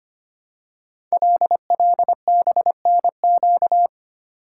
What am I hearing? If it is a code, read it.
Morse code, heard as a keyed tone, about 25 wpm, 700 Hz. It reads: LLBNQ
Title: Code Groups - 5 Letters - 25wpm